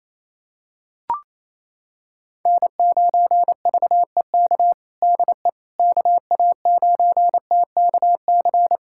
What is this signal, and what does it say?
Morse code, heard as a keyed tone, about 28 wpm, 700 Hz. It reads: N9VEK DE KA9TKC
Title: USA Callsign DE Callsign 28wpm